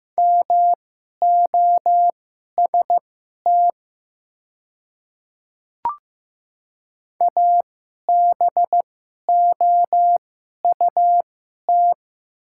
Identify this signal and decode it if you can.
Morse code, heard as a keyed tone, about 15 wpm, spaced as for 11 wpm, 700 Hz. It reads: MOST ABOUT
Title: 2x Character Spacing - Top 100 Words - 15wpm